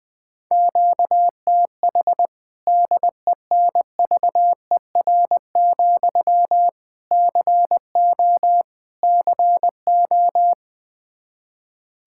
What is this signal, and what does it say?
Morse code, heard as a keyed tone, about 20 wpm, 700 Hz. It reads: QTH DENVER, CO CO
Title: Practice QSO - Exchange 1 - Beginning - 20wpm